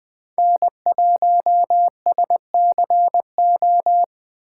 Morse code, heard as a keyed tone, about 20 wpm, 700 Hz. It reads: N1SCO
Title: Speed Racing - USA 1x3 Call Signs 17wpm